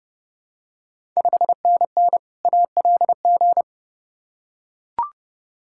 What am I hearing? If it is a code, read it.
Morse code, heard as a keyed tone, about 30 wpm, 700 Hz. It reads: 5NN ALG
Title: Ontario QSO Party Exchange 30wpm